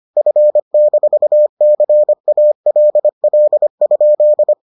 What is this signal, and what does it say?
Morse code, heard as a keyed tone, about 25 wpm, 600 Hz. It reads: F-CALL?